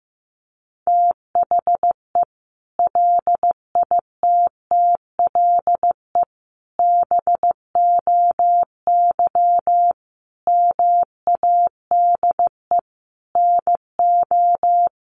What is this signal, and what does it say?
Morse code, heard as a keyed tone, about 15 wpm, 700 Hz. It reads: THE LITTLE BOY MADE NO
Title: Sentences from Top 1400 Words 15wpm